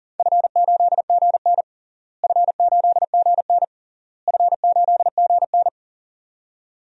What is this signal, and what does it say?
Morse code, heard as a keyed tone, about 40 wpm, 700 Hz. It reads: F8GN F8GN F8GN
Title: DXCC-100 Call Signs - 4 Characters - Repeated 3x - 40wpm